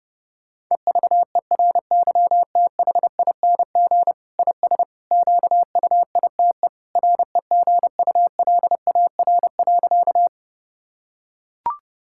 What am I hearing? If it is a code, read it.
Morse code, heard as a keyed tone, about 30 wpm, 700 Hz. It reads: EVERYTHING IS QUITE REGULAR.
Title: Sentences from Top 1000 Words - Review Part 6 - 30wpm